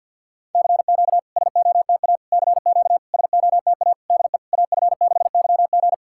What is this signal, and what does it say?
Morse code, heard as a keyed tone, about 50 wpm, 700 Hz. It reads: CQ SOTA CQ SOTA DE AF6YG
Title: SOTA QSO Practice 50wpm